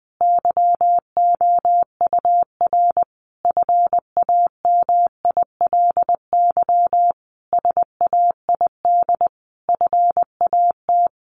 Morse code, heard as a keyed tone, about 20 wpm, 700 Hz. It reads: YOUR FAMILY SAID FAT
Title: Sentences from Top 1100 Words 20wpm